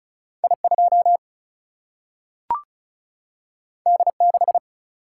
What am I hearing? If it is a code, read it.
Morse code, heard as a keyed tone, about 36 wpm, 700 Hz. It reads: I2 D6
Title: Single-Letter Single-Number 35wpm